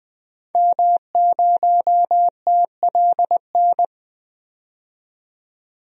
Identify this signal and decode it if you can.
Morse code, heard as a keyed tone, about 20 wpm, 700 Hz. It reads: M0TLN
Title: DXCC-100 Call Signs - 5 Characters - 20wpm